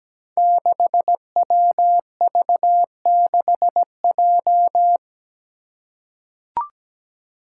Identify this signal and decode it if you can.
Morse code, heard as a keyed tone, about 17 wpm, 700 Hz. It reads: 6WV6J
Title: Five Challenging Letter-Number Combinations 17wpm